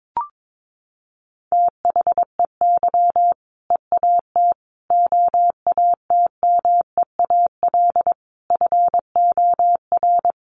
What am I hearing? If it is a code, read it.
Morse code, heard as a keyed tone, about 22 wpm, 700 Hz. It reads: THEY EAT OATMEAL FOR
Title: Easy Phrases 22wpm